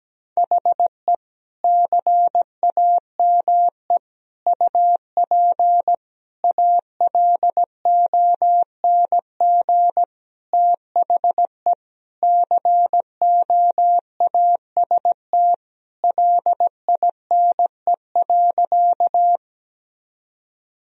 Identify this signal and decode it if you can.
Morse code, heard as a keyed tone, about 17 wpm, 700 Hz. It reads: HE CAME UP ALONG THE COAST LINE.
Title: Sentences from Top 1000 Words - Review Part 4 - 17wpm